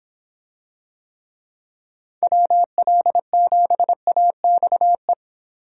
Morse code, heard as a keyed tone, about 26 wpm, 700 Hz. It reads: WL7AXE